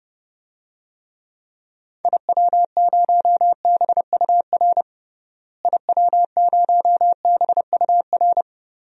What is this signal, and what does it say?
Morse code, heard as a keyed tone, about 30 wpm, 700 Hz. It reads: IW0BUR IW0BUR
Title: DXCC-100 Call Signs - 6 Characters - Repeated 3x - 30wpm